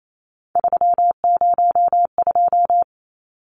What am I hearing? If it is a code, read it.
Morse code, heard as a keyed tone, about 28 wpm, 700 Hz. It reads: 302
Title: Numbers - 3-Digit 28wpm